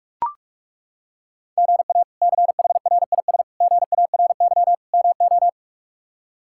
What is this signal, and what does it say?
Morse code, heard as a keyed tone, about 41 wpm, 700 Hz. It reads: GA CHRIS GARY MO
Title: K1USN SST QSO Practice - 45wpm